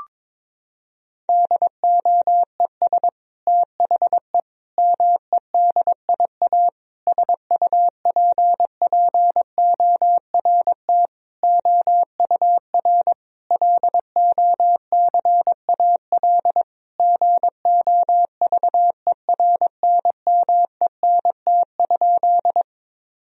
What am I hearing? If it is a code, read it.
Morse code, heard as a keyed tone, about 22 wpm, 700 Hz. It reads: DOES THE MEDIA SUPPORT OUR LOCAL GOVERNMENT?
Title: Sentences from Top 1000 Words - Review Part 4 - 22wpm